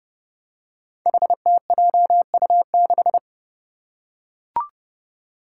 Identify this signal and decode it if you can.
Morse code, heard as a keyed tone, about 29 wpm, 700 Hz. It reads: HTJU6
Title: Five Challenging Letter-Number Combinations 30wpm